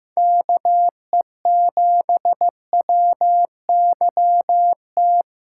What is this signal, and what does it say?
Morse code, heard as a keyed tone, about 16 wpm, 700 Hz. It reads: KE7WYT
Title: USA Callsign DE Callsign 15wpm